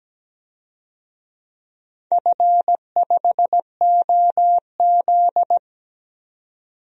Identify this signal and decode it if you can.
Morse code, heard as a keyed tone, about 17 wpm, 700 Hz. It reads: F5OZ
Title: DXCC-100 Call Signs - 4 Characters - 17wpm